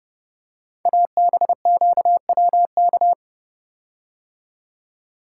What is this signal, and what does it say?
Morse code, heard as a keyed tone, about 30 wpm, 700 Hz. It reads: ABQWK
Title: Code Groups - 5 Letters - 30wpm